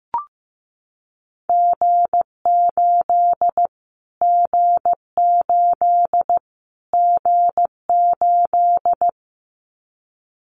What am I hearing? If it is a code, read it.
Morse code, heard as a keyed tone, about 15 wpm, 700 Hz. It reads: G8 G8 G8
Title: Callsign Prefixes for Top 25 DXCC - Repeated 3x - 15wpm